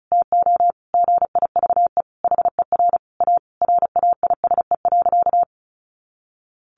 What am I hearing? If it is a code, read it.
Morse code, heard as a keyed tone, about 34 wpm, 700 Hz. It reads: TO GIVE HER A RAISE.
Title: Easy Phrases 35wpm